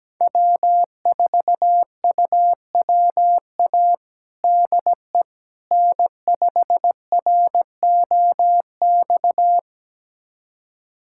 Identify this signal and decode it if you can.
Morse code, heard as a keyed tone, about 17 wpm, 700 Hz. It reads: W4UWA DE N5ROX
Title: USA Callsign DE Callsign - 5 Characters - 17wpm